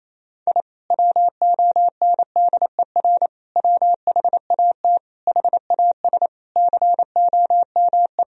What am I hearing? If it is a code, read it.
Morse code, heard as a keyed tone, about 28 wpm, 700 Hz. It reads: I WONDER WHAT HAS COME
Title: Sentences from Top 1000 Words - Review Part 4 - 28wpm